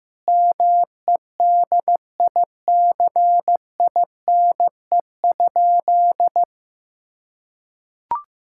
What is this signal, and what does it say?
Morse code, heard as a keyed tone, about 15 wpm, 700 Hz. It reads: MEDICINE?
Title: Sentences from Top 1000 Words - Review Part 6 - 15wpm